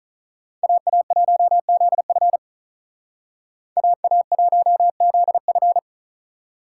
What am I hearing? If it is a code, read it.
Morse code, heard as a keyed tone, about 45 wpm, 700 Hz. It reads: AA1ZF AA1ZF